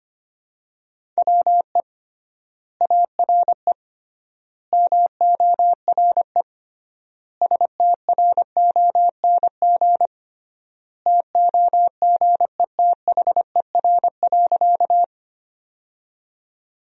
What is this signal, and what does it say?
Morse code, heard as a keyed tone, about 25 wpm, 700 Hz. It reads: WE ARE MORE STRONG TOGETHER.
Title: 3x Word-Spacing - Sentences from Top 500 Words - 25wpm